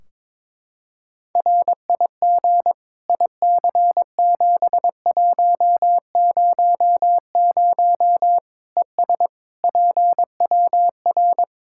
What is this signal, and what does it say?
Morse code, heard as a keyed tone, about 22 wpm, 700 Hz. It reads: RIG IC7100 ES PWR
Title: K1USN SST QSO Practice 22wpm